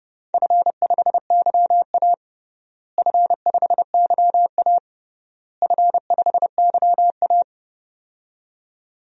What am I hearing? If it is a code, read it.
Morse code, heard as a keyed tone, about 32 wpm, 700 Hz. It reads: F5YA F5YA F5YA
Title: DXCC-100 Call Signs - 4 Characters - Repeated 3x - 30wpm